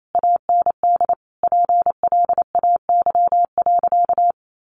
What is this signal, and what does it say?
Morse code, heard as a keyed tone, about 28 wpm, 700 Hz. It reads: AND PLAY.
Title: Sentences from Top 1300 Words 28wpm